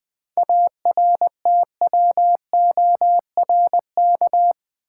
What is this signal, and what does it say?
Morse code, heard as a keyed tone, about 20 wpm, 700 Hz. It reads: ARTWORK
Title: 7 Letter Words - First 3 Letters Given - 20wpm